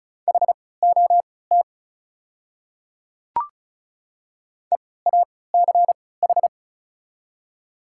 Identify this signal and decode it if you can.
Morse code, heard as a keyed tone, about 34 wpm, spaced as for 20 wpm, 690 Hz. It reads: HOT EACH
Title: 3x Character Spacing - Top 100 Words - 35wpm